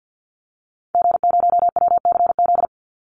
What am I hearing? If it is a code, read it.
Morse code, heard as a keyed tone, about 50 wpm, 700 Hz. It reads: G0WCZ
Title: Call Signs - 50wpm